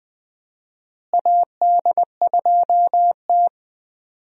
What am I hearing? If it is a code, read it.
Morse code, heard as a keyed tone, about 20 wpm, 700 Hz. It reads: AD2T
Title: USA Call Signs - 4 Characters - Repeated 3x - 20wpm